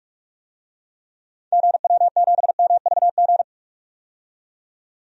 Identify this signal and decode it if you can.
Morse code, heard as a keyed tone, about 46 wpm, 700 Hz. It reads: GW7MVG
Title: DXCC-100 Call Signs - 6 Characters - 45wpm